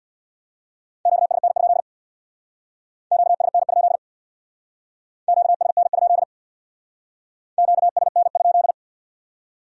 Morse code, heard as a keyed tone, about 63 wpm, 700 Hz. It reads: QSN? QSN? QSN? QSN?